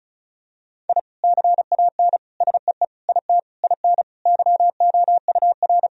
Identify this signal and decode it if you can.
Morse code, heard as a keyed tone, about 36 wpm, 700 Hz. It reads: I CAN SEE IT IN YOUR
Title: Sentences from Top 1000 Words - Review Part 2 - 35wpm